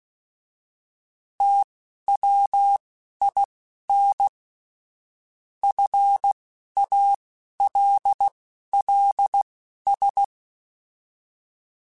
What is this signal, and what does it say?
Morse code, heard as a keyed tone, about 16 wpm, spaced as for 11 wpm, 800 Hz. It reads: TWIN FALLS